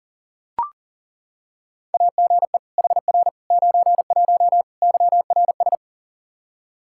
Morse code, heard as a keyed tone, about 40 wpm, 700 Hz. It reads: AGE HR 91 YRS
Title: K1USN SST QSO Practice 40wpm